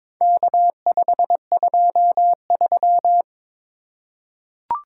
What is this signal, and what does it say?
Morse code, heard as a keyed tone, about 22 wpm, 700 Hz. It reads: K523
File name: practice / 22wpm / Four Letter-Number Combinations - 22wpm